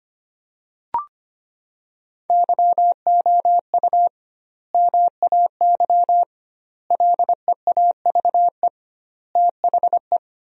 Morse code, heard as a keyed tone, about 25 wpm, 700 Hz. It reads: YOU MAY LEAVE THE
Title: 2x Word-Spacing - Sentences from Top 400 Words - 25wpm